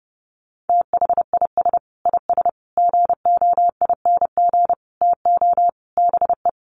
Morse code, heard as a keyed tone, about 30 wpm, 700 Hz. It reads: THIS IS GOING TO BE
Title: Sets of 5 Words 30wpm